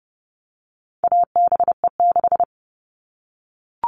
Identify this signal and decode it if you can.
Morse code, heard as a keyed tone, about 30 wpm, 700 Hz. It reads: ABE6
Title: All Cut Numbers - 4-Digit 30wpm